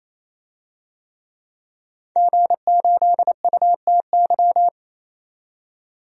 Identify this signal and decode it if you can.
Morse code, heard as a keyed tone, about 28 wpm, 700 Hz. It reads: G8UTY
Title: DXCC-100 Call Signs - 5 Characters - 28wpm